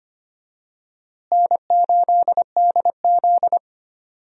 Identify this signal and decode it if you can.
Morse code, heard as a keyed tone, about 25 wpm, 700 Hz. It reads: N8DZ
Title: USA Call Signs - 4 Characters - Repeated 2x - 25wpm